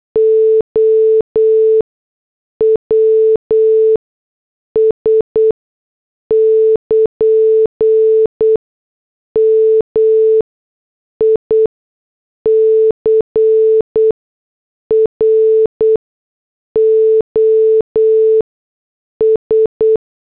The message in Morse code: OWS <KN> MICROS